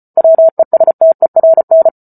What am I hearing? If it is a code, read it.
Morse code, heard as a keyed tone, about 34 wpm, 650 Hz. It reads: WESTERN
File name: Book Two / 08-Chapter 8 - 34 WPM